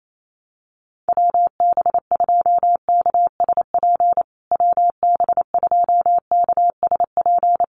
Morse code, heard as a keyed tone, about 28 wpm, 700 Hz. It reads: WB2KSP WB2KSP
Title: USA Call Signs Repeated 3x 28wpm